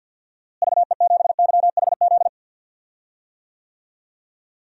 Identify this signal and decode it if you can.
Morse code, heard as a keyed tone, about 53 wpm, 700 Hz. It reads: VE7YHZ
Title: DXCC-100 Call Signs - 6 Characters - Repeated 3x - 50wpm